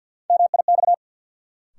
Morse code, heard as a keyed tone, about 48 wpm, 700 Hz. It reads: MIX
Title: Sentences from Top 1000 Words - Review Part 4 - New Word Drill - 50wpm